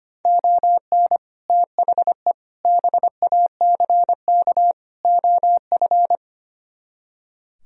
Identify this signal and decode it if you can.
Morse code, heard as a keyed tone, about 31 wpm, 700 Hz. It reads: ON THE BACK OF
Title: Speed-Racing - Sets of 4 Words 25wpm